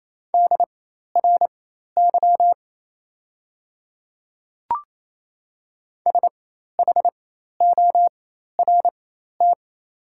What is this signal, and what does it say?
Morse code, heard as a keyed tone, about 28 wpm, spaced as for 13 wpm, 700 Hz. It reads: DRY SHORT
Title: 4x Character Spacing - Top 500 Words - 28wpm